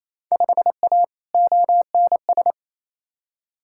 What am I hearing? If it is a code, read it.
Morse code, heard as a keyed tone, about 28 wpm, 700 Hz. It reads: 5A ONS
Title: Field Day Exchanges Encore - 28wpm